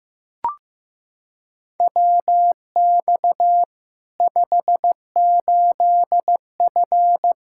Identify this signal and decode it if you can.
Morse code, heard as a keyed tone, about 15 wpm, 700 Hz. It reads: WX 58F
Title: Practice QSO - Exchange 2 - Info - 15wpm